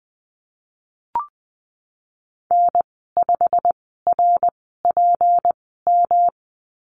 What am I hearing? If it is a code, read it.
Morse code, heard as a keyed tone, about 20 wpm, spaced as for 15 wpm, 700 Hz. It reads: N5RPM
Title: 213 Lesson 34g - Number 8 - Call Signs 20wpm